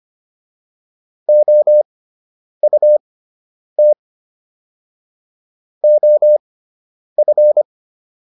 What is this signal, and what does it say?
Morse code, heard as a keyed tone, about 25 wpm, spaced as for 9 wpm, 600 Hz. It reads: OUT OF